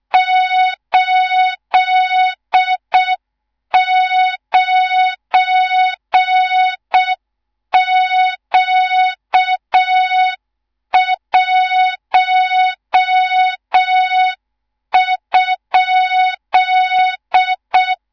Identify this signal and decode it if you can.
Morse code, heard as a keyed tone, about 6 wpm, 3700 Hz, with noise. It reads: 89Q1?
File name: Morze-HA2MN / hang / 18 IRASJELEK2